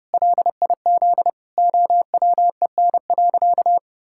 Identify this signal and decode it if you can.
Morse code, heard as a keyed tone, about 32 wpm, 700 Hz. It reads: LIZ OWEN.
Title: Pangram Challenge v2 - 30wpm